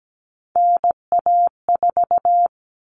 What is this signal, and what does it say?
Morse code, heard as a keyed tone, about 17 wpm, 700 Hz. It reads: NA4
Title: All Cut Numbers - 3-Digit 17wpm